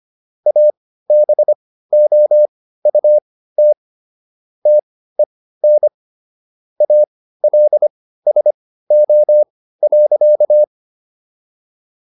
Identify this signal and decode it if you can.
Morse code, heard as a keyed tone, about 25 wpm, spaced as for 15 wpm, 600 Hz. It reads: ABOUT TEN ALSO.